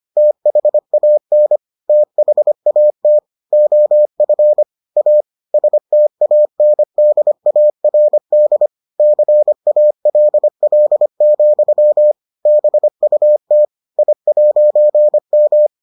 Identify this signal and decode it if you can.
Morse code, heard as a keyed tone, about 25 wpm, 600 Hz. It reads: THAN THAT OF A STANDARD CALL, BUT I'M